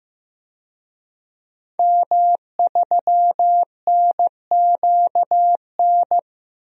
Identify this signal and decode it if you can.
Morse code, heard as a keyed tone, about 15 wpm, 700 Hz. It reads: M3NQN
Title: DXCC-100 Call Signs - 5 Characters - Repeated 2x - 15wpm